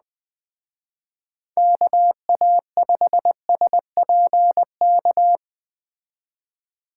Morse code, heard as a keyed tone, about 20 wpm, 700 Hz. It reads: KA5SPK
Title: Speed Racing - USA 2x3 Call Signs 17wpm